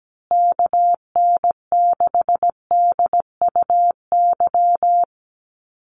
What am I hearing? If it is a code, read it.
Morse code, heard as a keyed tone, about 17 wpm, 700 Hz. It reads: KN6DUY
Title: USA Call Signs Repeated 3x 17wpm